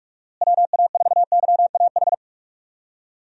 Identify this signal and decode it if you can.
Morse code, heard as a keyed tone, about 43 wpm, 700 Hz. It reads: WA4YAH
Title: USA Call Signs - 6 Characters - Repeated 2x - 45wpm